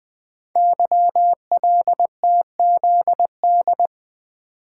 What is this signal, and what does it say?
Morse code, heard as a keyed tone, about 20 wpm, 700 Hz. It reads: YLTZD
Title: Code Groups - 5 Letter:Numbers - 20wpm